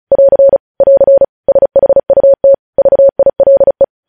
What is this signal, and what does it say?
Morse code, heard as a keyed tone, about 36 wpm, 550 Hz. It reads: <AR> <AR> SHUT VILE
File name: podcast / News headlines-1x-35x35